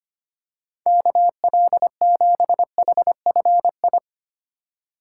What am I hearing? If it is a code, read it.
Morse code, heard as a keyed tone, about 25 wpm, 700 Hz. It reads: KL7HFI
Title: K1USN SST QSO Practice - 25wpm